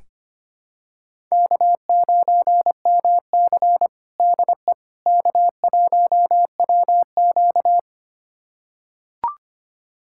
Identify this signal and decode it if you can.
Morse code, heard as a keyed tone, about 25 wpm, 700 Hz. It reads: K9MC DE K1WQ
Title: USA Callsign DE Callsign - 4 Characters - 25wpm